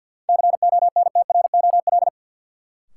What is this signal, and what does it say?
Morse code, heard as a keyed tone, about 49 wpm, 700 Hz. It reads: CONTROL
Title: Sentences from Top 1200 Words - New Word Drill Alt - 50wpm